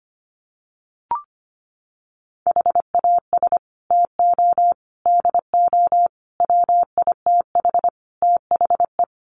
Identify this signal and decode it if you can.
Morse code, heard as a keyed tone, about 25 wpm, 700 Hz. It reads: HAS TO DO WITH THE
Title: Sets of 5 Words 25wpm